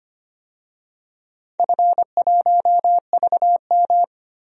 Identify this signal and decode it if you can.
Morse code, heard as a keyed tone, about 25 wpm, 700 Hz. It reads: F1VM
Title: DXCC-100 Call Signs - 4 Characters - 25wpm